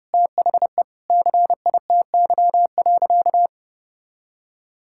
Morse code, heard as a keyed tone, about 30 wpm, 700 Hz. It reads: THE CITY.
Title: Sentences from Top 1000 Words - Review Part 6 - 30wpm